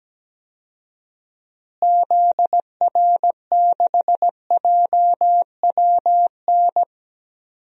Morse code, heard as a keyed tone, about 17 wpm, 700 Hz. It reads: ZR6JWN
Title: DXCC-100 Call Signs - 6 Characters - Repeated 2x - 17wpm